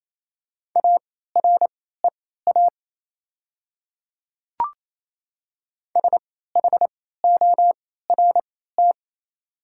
Morse code, heard as a keyed tone, about 28 wpm, spaced as for 16 wpm, 700 Hz. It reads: AREA SHORT
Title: 3x Character Spacing - Top 500 Words - 28wpm